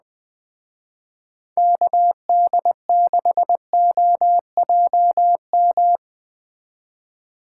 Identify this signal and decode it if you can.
Morse code, heard as a keyed tone, about 20 wpm, 700 Hz. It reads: KD6OJM